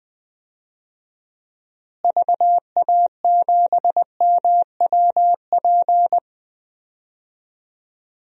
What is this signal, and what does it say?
Morse code, heard as a keyed tone, about 20 wpm, 700 Hz. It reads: VA7MWP
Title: DXCC-100 Call Signs - 6 Characters - Repeated 2x - 20wpm